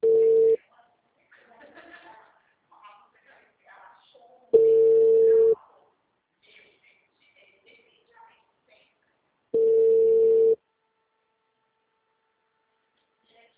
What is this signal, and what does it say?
Ringback tone (China, 450 Hz), with speech.
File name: out- -2002- - - .204